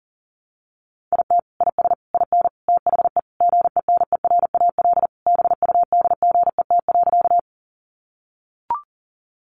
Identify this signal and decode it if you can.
Morse code, heard as a keyed tone, about 37 wpm, 700 Hz. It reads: IT IS IN THE GENERAL BUDGET.
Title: Sentences from Top 1200 Words 40wpm